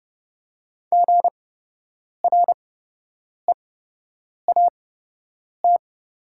Morse code, heard as a keyed tone, about 30 wpm, spaced as for 8 wpm, 700 Hz. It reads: GREAT
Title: 8x Character Spacing - Top 300 Words - 30wpm